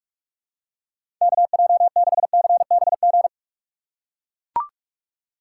Morse code, heard as a keyed tone, about 41 wpm, 700 Hz. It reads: KJ6CBG
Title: USA Call Signs - 6 Characters - 3x Character Spacing - 45wpm